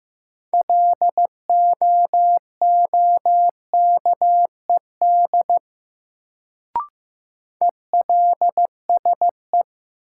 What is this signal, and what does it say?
Morse code, heard as a keyed tone, about 15 wpm, 700 Hz. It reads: LOOKED ELSE
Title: Sentences from Top 800 Words - New Word Drill - 15wpm